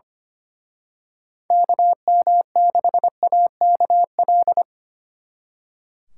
Morse code, heard as a keyed tone, about 29 wpm, 700 Hz. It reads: KM6AKL
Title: Speed Racing - USA 2x3 Call Signs 25wpm